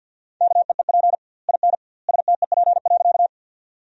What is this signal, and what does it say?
Morse code, heard as a keyed tone, about 50 wpm, 700 Hz. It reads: KEEP IN STEP.
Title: 2x Word-Spacing - Sentences from Top 400 Words - 50wpm